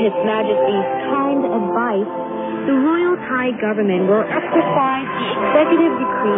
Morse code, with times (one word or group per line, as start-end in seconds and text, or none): none